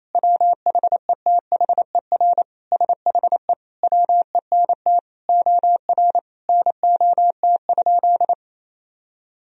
Morse code, heard as a keyed tone, about 28 wpm, 700 Hz. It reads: WHETHER SHE WENT OR NOT?
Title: Sentences from Top 1000 Words - Review Part 3 - 28wpm